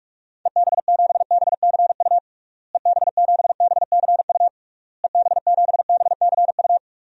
Morse code, heard as a keyed tone, about 46 wpm, 700 Hz. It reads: EB7BCU EB7BCU EB7BCU
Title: DXCC-100 Call Signs - 6 Characters - Repeated 3x - 45wpm